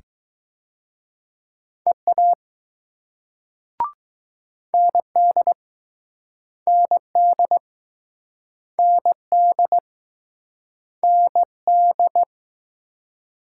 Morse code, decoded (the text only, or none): EA ND ND ND ND